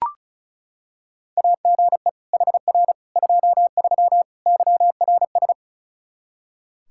Morse code, heard as a keyed tone, about 34 wpm, 700 Hz. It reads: AGE HR 23 YRS
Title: K1USN SST QSO Practice 35wpm